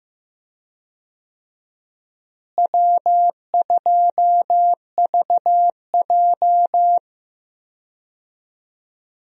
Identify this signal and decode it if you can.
Morse code, heard as a keyed tone, about 15 wpm, 700 Hz. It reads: W2VJ
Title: USA Call Signs - 4 Characters - Repeated 3x - 15wpm